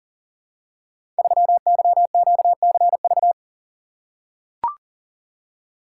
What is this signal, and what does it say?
Morse code, heard as a keyed tone, about 39 wpm, 700 Hz. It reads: 3YQCV
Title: Code Groups - 5 Letter:Numbers - 40wpm